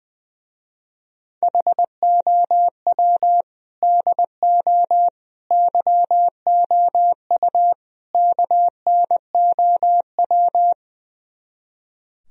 Morse code, heard as a keyed tone, about 20 wpm, 700 Hz. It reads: HOW DO YOU KNOW
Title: Speed-Racing - Sets of 4 Words 20wpm